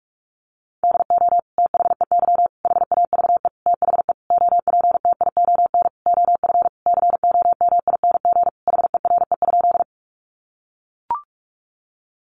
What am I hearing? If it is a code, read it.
Morse code, heard as a keyed tone, about 44 wpm, 700 Hz. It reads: DO THEY HAVE THE OPTION OF COMING HERE?
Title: Sentences from Top 1300 Words 45wpm